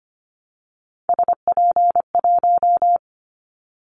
Morse code, heard as a keyed tone, about 25 wpm, 700 Hz. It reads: SP1
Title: Callsign Prefixes for Top 25 DXCC - 25wpm